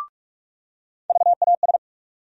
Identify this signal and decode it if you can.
Morse code, heard as a keyed tone, about 46 wpm, 700 Hz. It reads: VAS
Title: USA Call Sign Postfixes - 3 Letters 45wpm